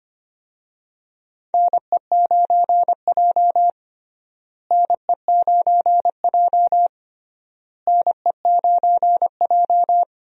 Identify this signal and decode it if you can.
Morse code, heard as a keyed tone, about 25 wpm, 700 Hz. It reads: NE9J NE9J NE9J
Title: USA Call Signs - 4 Characters - Repeated 3x - 25wpm